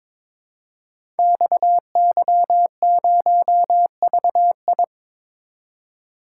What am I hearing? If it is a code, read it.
Morse code, heard as a keyed tone, about 22 wpm, 700 Hz. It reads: XY0VI
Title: Code Groups - 5 Letter:Numbers - 22wpm